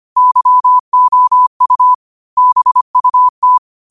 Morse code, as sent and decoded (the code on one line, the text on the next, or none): -.-- --- ..- / -... ..- -
YOU BUT